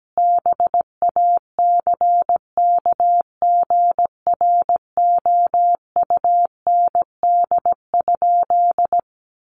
Morse code, heard as a keyed tone, about 17 wpm, 700 Hz. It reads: BACKGROUND?
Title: Sentences from Top 1300 Words 17wpm